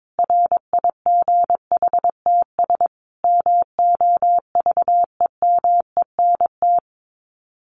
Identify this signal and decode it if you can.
Morse code, heard as a keyed tone, about 22 wpm, 700 Hz. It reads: RIGHTS MOVEMENT
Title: Sets of 5 Words Encore2 22wpm